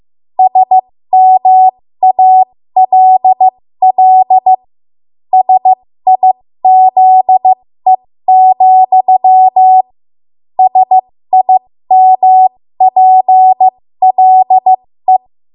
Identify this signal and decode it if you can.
Morse code, heard as a keyed tone, about 15 wpm, spaced as for 13 wpm, 750 Hz. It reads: SMALL SIZE, SIMPLE
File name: Morse / 13WPM